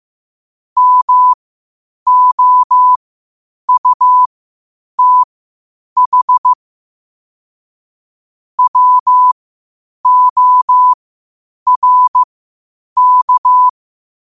Morse code, heard as a keyed tone, about 15 wpm, spaced as for 8 wpm, 1000 Hz. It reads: MOUTH WORK
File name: qotdfiles / 8wpm